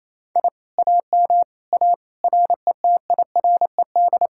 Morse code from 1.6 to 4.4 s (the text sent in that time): A RETIRED